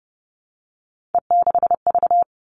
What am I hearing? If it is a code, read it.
Morse code, heard as a keyed tone, about 30 wpm, 700 Hz. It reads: E6V
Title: All Cut Numbers - 3-Digit 30wpm